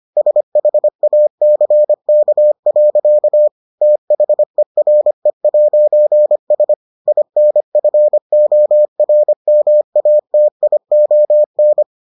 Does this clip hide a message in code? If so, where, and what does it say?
Yes: Morse code from 0.2 to 11.8 s, heard as a keyed tone, about 25 wpm, 600 Hz. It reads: SHACK. THERE'S INFORMATION